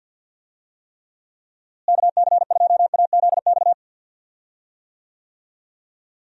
Morse code, heard as a keyed tone, about 47 wpm, 700 Hz. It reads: KC2AZX
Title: USA Call Signs - 6 Characters - Repeated 3x - 50wpm